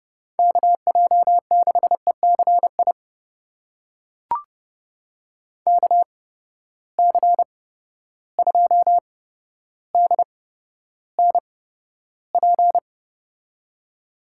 Morse code, heard as a keyed tone, about 32 wpm, spaced as for 13 wpm, 700 Hz. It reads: KJ6ECI KC2DNP
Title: USA Call Signs - 6 Characters - 8x Character Spacing - 30wpm